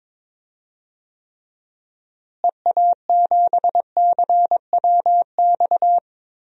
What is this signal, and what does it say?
Morse code, heard as a keyed tone, about 22 wpm, 700 Hz. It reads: EA7CWX
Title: DXCC-100 Call Signs - 6 Characters - Repeated 2x - 22wpm